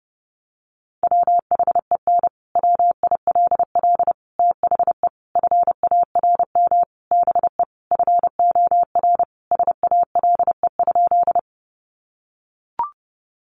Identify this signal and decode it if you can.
Morse code, heard as a keyed tone, about 30 wpm, 700 Hz. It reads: WHEN WILL THE FARM BE FOR SALE?
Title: Sentences from Top 1400 Words 30wpm